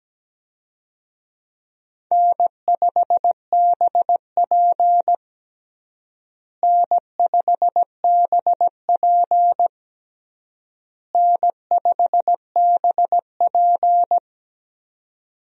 Morse code, heard as a keyed tone, about 17 wpm, 700 Hz. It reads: N5BP N5BP N5BP
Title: USA Call Signs - 4 Characters - Repeated 3x - 17wpm